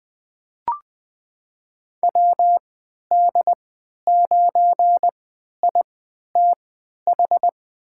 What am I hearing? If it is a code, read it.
Morse code, heard as a keyed tone, about 20 wpm, spaced as for 11 wpm, 700 Hz. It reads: WD9ITH
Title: USA Call Signs - 6 Characters - 3x Character Spacing - 20wpm